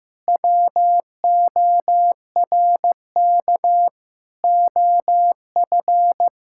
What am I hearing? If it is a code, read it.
Morse code, heard as a keyed tone, about 15 wpm, 700 Hz. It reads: WORK OF
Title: Sentences from Top 1000 Words - Review Part 6 - 15wpm